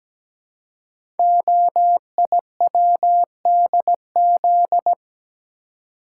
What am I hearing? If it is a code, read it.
Morse code, heard as a keyed tone, about 17 wpm, 700 Hz. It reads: OIWDZ
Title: Code Groups - 5 Letters - 17wpm